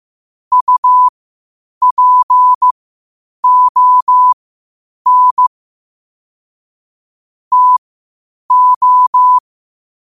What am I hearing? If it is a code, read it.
Morse code, heard as a keyed tone, about 15 wpm, spaced as for 8 wpm, 1000 Hz. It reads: UPON TO
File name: qotdfiles / 8wpm